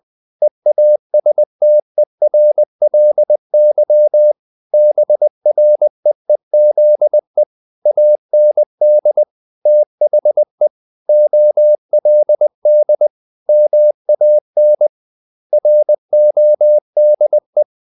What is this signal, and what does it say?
Morse code, heard as a keyed tone, about 20 wpm, 600 Hz. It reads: EASTERLY BREEZE AND THE OLD MAN RODE
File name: the old man and the sea / the old man and the sea part 1 20wpm